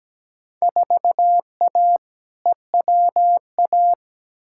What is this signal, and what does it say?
Morse code, heard as a keyed tone, about 17 wpm, 700 Hz. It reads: 4A EWA
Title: Field Day Exchanges - 17wpm